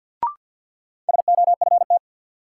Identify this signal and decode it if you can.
Morse code, heard as a keyed tone, about 53 wpm, 700 Hz. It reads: SOFT
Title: Sentences from Top 1500 Words - New Word Drill - 50wpm